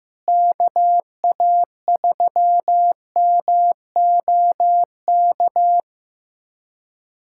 Morse code, heard as a keyed tone, about 15 wpm, 700 Hz. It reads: KA3MOK
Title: USA Callsign DE Callsign - 6 Characters - 15wpm